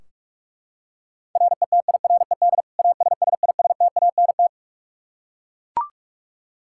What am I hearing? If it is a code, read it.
Morse code, heard as a keyed tone, about 42 wpm, 700 Hz. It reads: RETIRED ASSISTANT
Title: Practice QSO - Exchange 3 - More Info - 45wpm